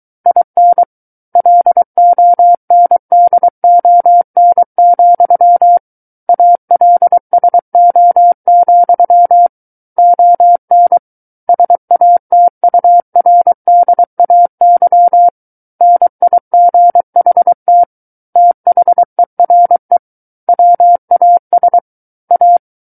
Morse code, heard as a keyed tone, about 23 wpm, 700 Hz. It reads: IN LONDON, ALSO, ON SATURDAY NIGHT THERE WAS A
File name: Book One / 14-Chapter 14 - 23 WPM